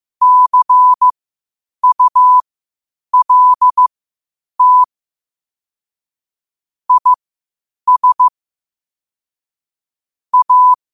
Morse code, heard as a keyed tone, about 15 wpm, spaced as for 8 wpm, 1000 Hz. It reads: CULT IS A
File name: qotdfiles / 8wpm